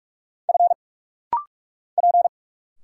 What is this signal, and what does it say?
Morse code, heard as a keyed tone, about 44 wpm, 700 Hz. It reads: F P
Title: Single Letters - Rapid-Fire 45wpm